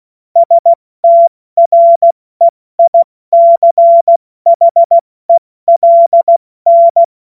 Morse code, heard as a keyed tone, about 16 wpm, 680 Hz. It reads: STREICHELN
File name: Witz-22-Juli-2025-70-F80